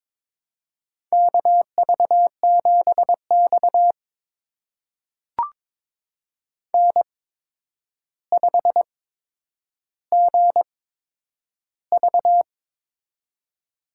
Morse code, heard as a keyed tone, about 22 wpm, spaced as for 9 wpm, 700 Hz. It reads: KV7X N5GV
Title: USA Call Signs - 4 Characters - 8x Character Spacing - 22wpm